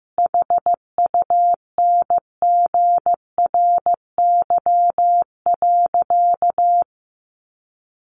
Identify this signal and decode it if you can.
Morse code, heard as a keyed tone, about 15 wpm, 700 Hz. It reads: HUNGRY.
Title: A Sound Of Thunder 15wpm